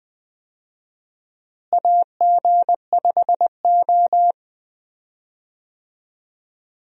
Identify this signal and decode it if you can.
Morse code, heard as a keyed tone, about 20 wpm, 700 Hz. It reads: AG5O
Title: USA Call Signs - 4 Characters - Repeated 2x - 20wpm